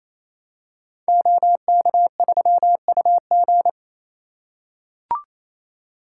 Morse code, heard as a keyed tone, about 28 wpm, 700 Hz. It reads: OK3UG
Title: Code Groups - 5 Letter:Numbers - 28wpm